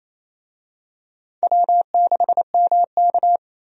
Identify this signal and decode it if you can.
Morse code, heard as a keyed tone, about 27 wpm, 700 Hz. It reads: W6MK
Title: USA Call Signs - 4 Characters - 28wpm